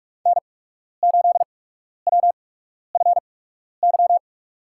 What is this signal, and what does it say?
Morse code, heard as a keyed tone, about 46 wpm, spaced as for 12 wpm, 700 Hz. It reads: N8WFY
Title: USA Call Signs - 5 Characters - 8x Character Spacing - 45wpm